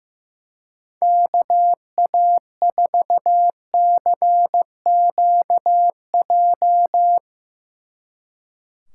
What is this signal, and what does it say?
Morse code, heard as a keyed tone, about 16 wpm, 700 Hz. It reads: KA4CQJ
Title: Speed Racing - USA 2x3 Call Signs 15wpm